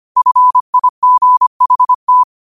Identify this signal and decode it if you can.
Morse code, heard as a keyed tone, about 25 wpm, 1000 Hz. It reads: FIGHT
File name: qotdfiles / 25wpm